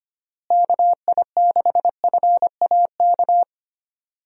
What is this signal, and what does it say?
Morse code, heard as a keyed tone, about 25 wpm, 700 Hz. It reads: KI6FAK
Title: USA Call Signs - 6 Characters - Repeated 3x - 25wpm